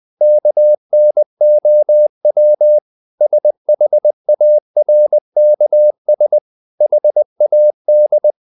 Morse code, heard as a keyed tone, about 20 wpm, 600 Hz. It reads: KNOW SHARKS HAD
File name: the old man and the sea part 2 20wpm